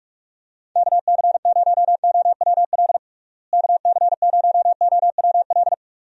Morse code, heard as a keyed tone, about 46 wpm, 700 Hz. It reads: KC0OWL KC0OWL
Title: USA Call Signs - 6 Characters - Repeated 3x - 45wpm